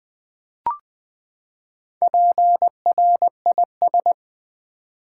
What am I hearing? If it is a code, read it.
Morse code, heard as a keyed tone, about 20 wpm, 700 Hz. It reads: PRIS
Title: Build A Word - 4 Letters At A Time - 8-Letter Words - 20wpm